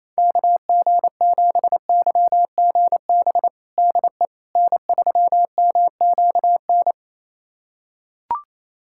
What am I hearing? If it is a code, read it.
Morse code, heard as a keyed tone, about 28 wpm, 700 Hz. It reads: KG7YGB DE N3MQN
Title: USA Callsign DE Callsign 28wpm